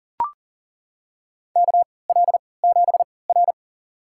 Morse code, heard as a keyed tone, about 40 wpm, spaced as for 23 wpm, 700 Hz. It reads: KL7R
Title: USA Call Signs - 4 Characters - 3x Character Spacing - 40wpm